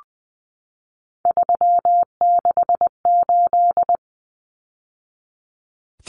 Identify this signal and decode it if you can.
Morse code, heard as a keyed tone, about 20 wpm, 700 Hz. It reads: 368
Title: Numbers - 3-Digit 20wpm